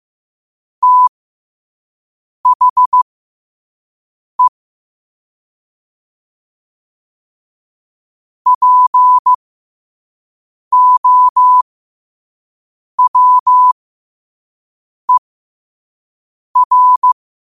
Morse code, heard as a keyed tone, about 15 wpm, spaced as for 5 wpm, 1000 Hz. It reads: THE POWER
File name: qotdfiles / 5wpm